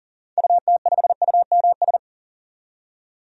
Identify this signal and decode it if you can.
Morse code, heard as a keyed tone, about 40 wpm, 700 Hz. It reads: UT5UMS
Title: DXCC-100 Call Signs - 6 Characters - 40wpm